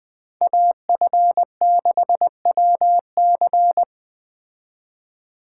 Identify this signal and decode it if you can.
Morse code, heard as a keyed tone, about 20 wpm, 700 Hz. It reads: AF6WC
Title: Speed Racing - USA 2x2 Call Signs 17wpm